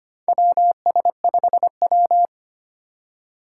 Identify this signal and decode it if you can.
Morse code, heard as a keyed tone, about 25 wpm, 700 Hz. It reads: WS5W
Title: Four Challenging Letter-Number Combinations 25wpm